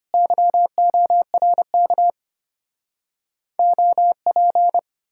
Morse code, heard as a keyed tone, about 33 wpm, 700 Hz. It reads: YORK OP